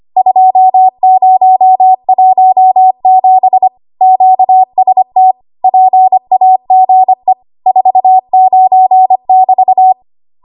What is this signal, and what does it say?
Morse code, heard as a keyed tone, about 25 wpm, 750 Hz. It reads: 2017 QST PAGE 49 <BT>
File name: Archive / 25 WPM / 25WPM